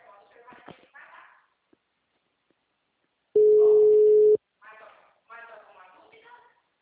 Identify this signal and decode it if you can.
Ringback tone (Europe, 425 Hz), with speech.